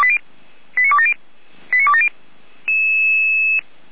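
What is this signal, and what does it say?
Five-tone selective call (ZVEI), heard as quick runs of five tones, tones of about 70 ms. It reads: 89270 89270
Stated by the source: ZVEI five-tone selective call 89001 received on SDR.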